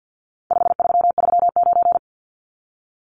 Morse code, heard as a keyed tone, about 48 wpm, 700 Hz. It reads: 5339
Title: Numbers - 4-Digit 50wpm